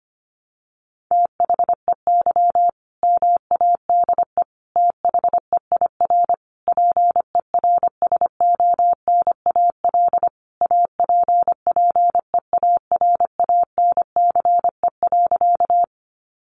Morse code, heard as a keyed tone, about 25 wpm, 700 Hz. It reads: THEY MADE THEIR PERSONAL APPEARANCE.
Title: Sentences from Top 1100 Words 25wpm